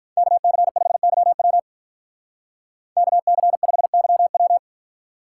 Morse code, heard as a keyed tone, about 43 wpm, 700 Hz. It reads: KC5YW KC5YW
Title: Speed Racing - USA Call Signs 35wpm